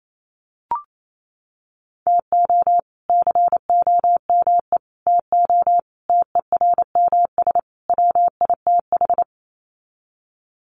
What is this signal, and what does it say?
Morse code, heard as a keyed tone, about 28 wpm, 700 Hz. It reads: TO COME TO TERMS WITH
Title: Sets of 5 Words Encore 28wpm